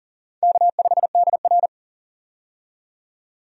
Morse code, heard as a keyed tone, about 40 wpm, 700 Hz. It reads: K5DR
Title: USA Call Signs - 4 Characters - 40wpm